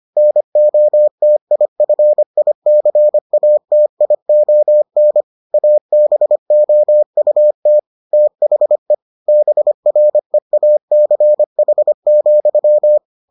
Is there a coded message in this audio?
Yes: Morse code from 0.2 to 13.0 s, heard as a keyed tone, about 25 wpm, 600 Hz. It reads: NOTIFICATION ABOUT THE BREACH,